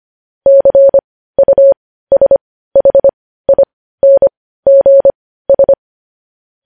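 Morse code, heard as a keyed tone, about 25 wpm, spaced as for 15 wpm, 550 Hz. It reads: CUSHINGS
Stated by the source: Morse practice sending news headlines at 15 wpm.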